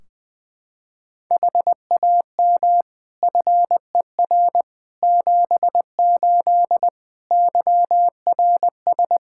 Morse code, heard as a keyed tone, about 20 wpm, 700 Hz. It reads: HAM FER 78 YRS
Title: Practice QSO - Exchange 3 - More Info - 20wpm